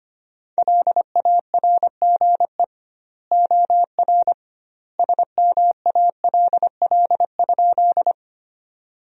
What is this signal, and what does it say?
Morse code, heard as a keyed tone, about 25 wpm, 700 Hz. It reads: LARGE OR SMALL?
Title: 2x Word-Spacing - Sentences from Top 200 Words - 25wpm